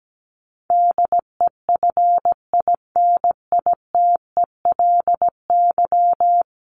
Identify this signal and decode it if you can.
Morse code, heard as a keyed tone, about 17 wpm, 700 Hz. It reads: DEFINITELY
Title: Top 1751-2000 Words 17wpm